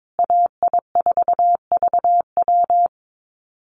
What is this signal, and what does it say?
Morse code, heard as a keyed tone, about 22 wpm, 700 Hz. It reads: AI4VW
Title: USA Call Signs Repeated 3x 22wpm